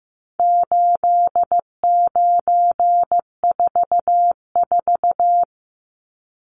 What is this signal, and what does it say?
Morse code, heard as a keyed tone, about 15 wpm, 700 Hz. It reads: 8944
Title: Numbers - 4-Digit 15wpm